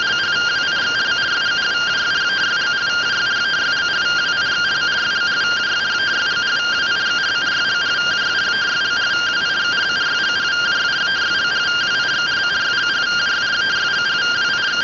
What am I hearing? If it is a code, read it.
Two-tone teleprinter signal, about 170 Hz shift.